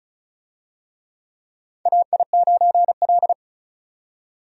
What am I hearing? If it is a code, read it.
Morse code, heard as a keyed tone, about 36 wpm, 700 Hz. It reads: AI9L
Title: USA Call Signs - 4 Characters - 35wpm